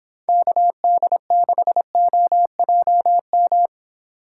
Morse code, heard as a keyed tone, about 26 wpm, 700 Hz. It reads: KD6OJM